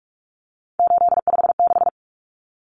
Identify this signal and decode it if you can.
Morse code, heard as a keyed tone, about 44 wpm, 700 Hz. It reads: 856
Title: Numbers - 3-Digit 45wpm